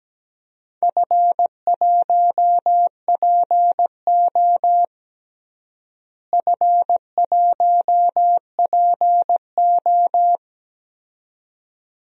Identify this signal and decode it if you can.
Morse code, heard as a keyed tone, about 17 wpm, 700 Hz. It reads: F1PO F1PO
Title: DXCC-100 Call Signs - 4 Characters - Repeated 2x - 17wpm